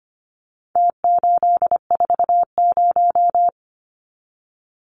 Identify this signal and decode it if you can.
Morse code, heard as a keyed tone, about 25 wpm, 700 Hz. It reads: T840
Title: Numbers - Contest Style - 4-Digit 25wpm